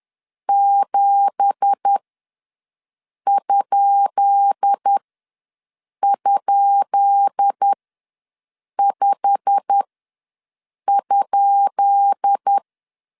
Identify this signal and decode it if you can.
Morse code, heard as a keyed tone, about 10 wpm, spaced as for 6 wpm, 800 Hz. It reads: 7??5?
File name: LEZIONI / 10 SIGNS / 10B INT ws 10wpm 4134